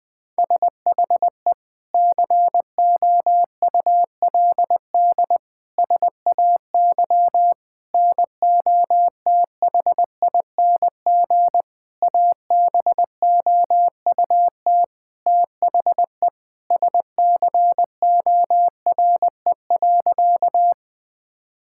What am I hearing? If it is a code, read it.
Morse code, heard as a keyed tone, about 20 wpm, 700 Hz. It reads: SHE COULD SAY NOTHING ABOUT THE SCORE.
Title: Sentences from Top 1600 Words 20wpm